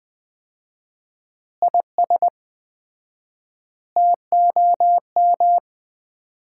Morse code, heard as a keyed tone, about 20 wpm, 700 Hz. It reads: IS TOM
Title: Common QSO Phrases - 4x Word-Spacing - 20wpm